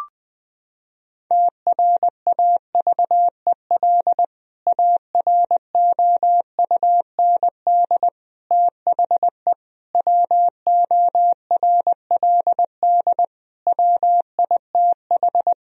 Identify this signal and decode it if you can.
Morse code, heard as a keyed tone, about 20 wpm, 700 Hz. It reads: TRAVEL AROUND THE WORLD WITH
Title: Sentences from Top 1000 Words - Review Part 3 - 20wpm